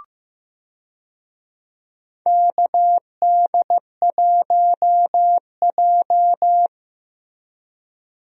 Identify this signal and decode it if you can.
Morse code, heard as a keyed tone, about 15 wpm, 700 Hz. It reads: KD1J
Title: USA Call Signs - 4 Characters - Repeated 2x - 15wpm